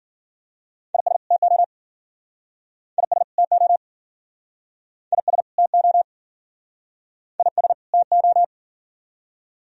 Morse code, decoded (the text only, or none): IS TO IS TO IS TO IS TO